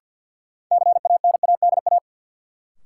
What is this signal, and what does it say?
Morse code, heard as a keyed tone, about 48 wpm, 700 Hz. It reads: CANADA
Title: Sets of 5 Words - New Word Drill - 50wpm